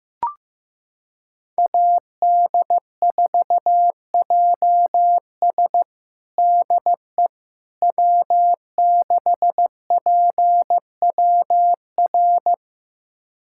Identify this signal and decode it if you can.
Morse code, heard as a keyed tone, about 15 wpm, 700 Hz. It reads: AD4JS DE W6PWR
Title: USA Callsign DE Callsign - 5 Characters - 15wpm